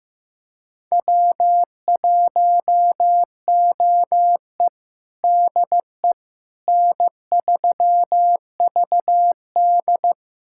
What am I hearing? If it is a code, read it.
Morse code, heard as a keyed tone, about 16 wpm, 700 Hz. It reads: W1OE DE N3VD
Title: USA Callsign DE Callsign - 4 Characters - 15wpm